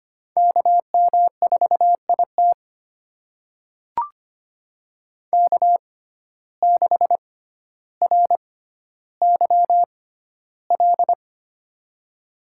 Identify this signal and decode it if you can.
Morse code, heard as a keyed tone, about 25 wpm, spaced as for 13 wpm, 700 Hz. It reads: KM4IT K6RYL
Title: USA Call Signs - 5 Characters - 6x Character Spacing - 25wpm